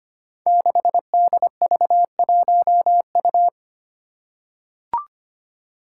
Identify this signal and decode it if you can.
Morse code, heard as a keyed tone, about 25 wpm, 700 Hz. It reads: 6DV1U
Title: Five Challenging Letter-Number Combinations 25wpm